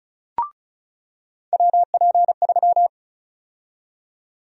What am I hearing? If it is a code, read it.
Morse code, heard as a keyed tone, about 36 wpm, 700 Hz. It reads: WP3
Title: USA Call Sign Prefixes - 35wpm